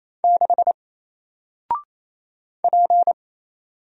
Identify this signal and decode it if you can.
Morse code, heard as a keyed tone, about 27 wpm, 700 Hz. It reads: 6 P
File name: Single-Letter-Number-28wpm